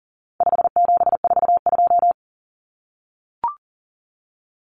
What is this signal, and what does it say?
Morse code, heard as a keyed tone, about 40 wpm, 700 Hz. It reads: 5742
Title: Common Cut Numbers - 4-Digit 40wpm